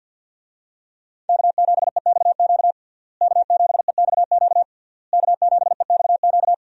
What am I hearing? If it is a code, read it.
Morse code, heard as a keyed tone, about 50 wpm, 700 Hz. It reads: K7EXQ K7EXQ K7EXQ
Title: USA Call Signs - 5 Characters - Repeated 3x - 50wpm